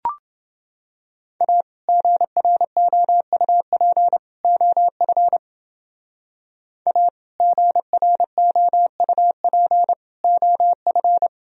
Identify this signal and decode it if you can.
Morse code, heard as a keyed tone, about 31 wpm, 700 Hz. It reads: A GROUP OF A GROUP OF